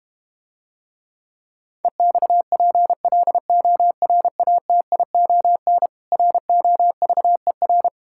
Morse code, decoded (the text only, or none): EXPLORATION ROVER